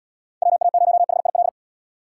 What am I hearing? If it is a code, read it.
Morse code, heard as a keyed tone, about 77 wpm, 700 Hz. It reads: WA0HIL